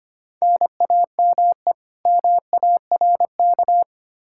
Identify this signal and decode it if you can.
Morse code, heard as a keyed tone, about 25 wpm, 700 Hz. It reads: NAME MARK
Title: Practice QSO - Exchange 1 - Beginning - 25wpm